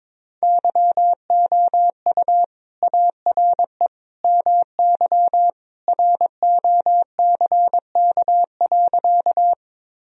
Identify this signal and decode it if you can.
Morse code, heard as a keyed tone, about 21 wpm, 700 Hz. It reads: YOU ARE MY ROCK.
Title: Sentences from Top 1000 Words - Review Part 2 - 22wpm